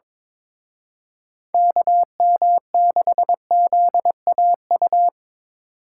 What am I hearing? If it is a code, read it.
Morse code, heard as a keyed tone, about 25 wpm, 700 Hz. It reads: KM6ZAU
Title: Speed Racing - USA 2x3 Call Signs 22wpm